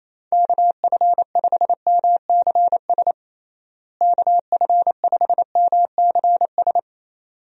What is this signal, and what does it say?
Morse code, heard as a keyed tone, about 28 wpm, 700 Hz. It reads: KF5MCS KF5MCS
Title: USA Call Signs - 6 Characters - Repeated 2x - 28wpm